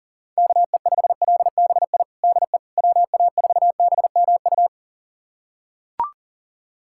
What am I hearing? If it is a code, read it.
Morse code, heard as a keyed tone, about 40 wpm, 700 Hz. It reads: KE5LBI DE WA4BMU
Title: USA Callsign DE Callsign - 6 Characters - 40wpm